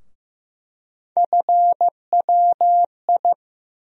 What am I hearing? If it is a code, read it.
Morse code, heard as a keyed tone, about 15 wpm, 700 Hz. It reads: FWI
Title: USA Call Sign Postfixes - 3 Letters 15wpm